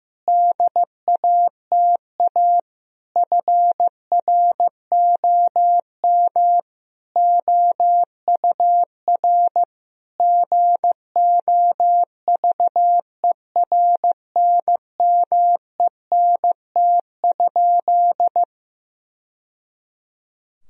Morse code, heard as a keyed tone, about 15 wpm, 700 Hz. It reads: DATA FROM OUR GOVERNMENT?
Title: Sentences from Top 1000 Words - Review Part 3 - 15wpm